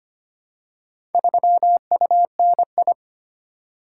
Morse code, heard as a keyed tone, about 25 wpm, 700 Hz. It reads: 3UNI
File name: practice / 25wpm / Four Letter-Number Combinations - 25wpm